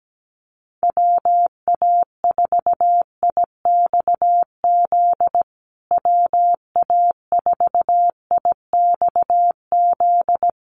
Morse code, heard as a keyed tone, about 17 wpm, 700 Hz. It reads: WA4IXZ WA4IXZ
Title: USA Call Signs Repeated 3x 17wpm